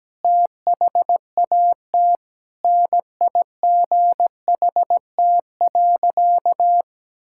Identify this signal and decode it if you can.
Morse code, heard as a keyed tone, about 17 wpm, 700 Hz. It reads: THAT NIGHT.
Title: Sentences from Top 1600 Words 17wpm